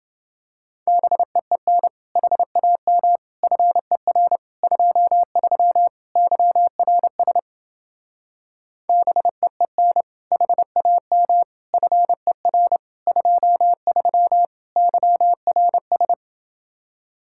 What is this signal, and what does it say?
Morse code, heard as a keyed tone, about 29 wpm, 700 Hz. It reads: BEEN HAM FER 23 YRS BEEN HAM FER 23 YRS